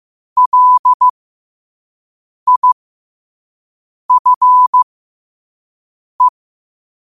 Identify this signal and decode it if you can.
Morse code, heard as a keyed tone, about 15 wpm, spaced as for 5 wpm, 1000 Hz. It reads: LIFE
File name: qotdfiles / 5wpm